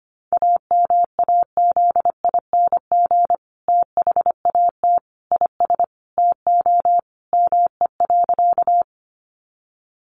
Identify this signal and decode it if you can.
Morse code, heard as a keyed tone, about 25 wpm, 700 Hz. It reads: AMAZING THAT IS TO ME.
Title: Sentences from Top 1500 Words 25wpm